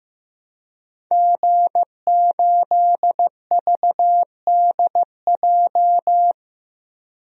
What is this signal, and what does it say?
Morse code, heard as a keyed tone, about 15 wpm, 700 Hz. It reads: G8VDJ
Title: DXCC-100 Call Signs - 5 Characters - Repeated 2x - 15wpm